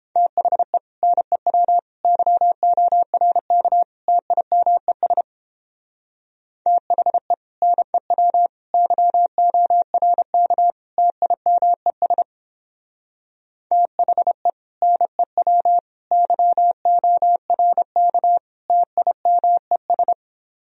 Morse code, heard as a keyed tone, about 30 wpm, 700 Hz. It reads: THE NEW YORK TIMES THE NEW YORK TIMES THE NEW YORK TIMES